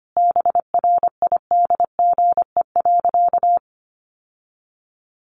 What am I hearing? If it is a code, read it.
Morse code, heard as a keyed tone, about 25 wpm, 700 Hz. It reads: BRIDGE.
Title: Sentences from Top 1200 Words 25wpm